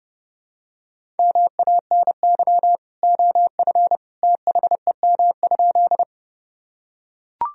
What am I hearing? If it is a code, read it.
Morse code, heard as a keyed tone, about 29 wpm, 700 Hz. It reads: MANY OF THEM?
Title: Sentences from Top 1000 Words - Review Part 1 - 30wpm